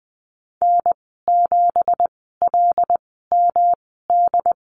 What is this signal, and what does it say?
Morse code, heard as a keyed tone, about 20 wpm, spaced as for 14 wpm, 700 Hz. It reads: N7LMD
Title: 167 Lesson 27g - Number 7 - Call Signs 20wpm